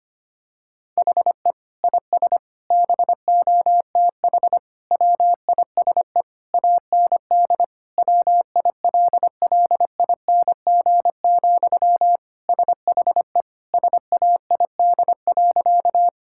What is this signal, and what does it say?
Morse code, heard as a keyed tone, about 25 wpm, 700 Hz. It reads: HE IS BOTH WISE AND WILLING, SHE SAID.
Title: Sentences from Top 1600 Words 25wpm